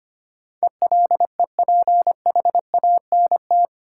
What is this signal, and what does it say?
Morse code, heard as a keyed tone, about 25 wpm, 690 Hz. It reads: ELEPHANT
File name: practice / 25wpm / Easy Words with Hints - 25wpm